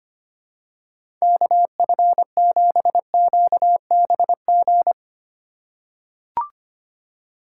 Morse code, heard as a keyed tone, about 25 wpm, 700 Hz. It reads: KF7QBG
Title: USA Call Signs - 3x Character Spacing - 25wpm